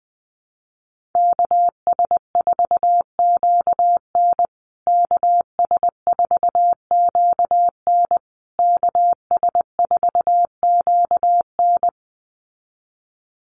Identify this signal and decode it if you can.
Morse code, heard as a keyed tone, about 20 wpm, 700 Hz. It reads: KS4QN KS4QN KS4QN
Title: USA Call Signs Repeated 3x 20wpm